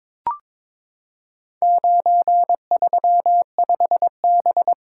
Morse code, heard as a keyed tone, about 22 wpm, 700 Hz. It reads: 935B
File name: practice / 22wpm / Four Letter-Number Combinations - 22wpm